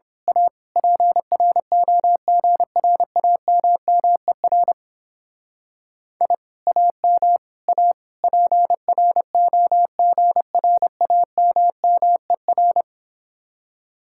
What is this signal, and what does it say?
Morse code, heard as a keyed tone, about 26 wpm, 700 Hz. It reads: A PROGRAMMER I AM A PROGRAMMER
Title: Speed-Racing Common QSO Phrases - 22wpm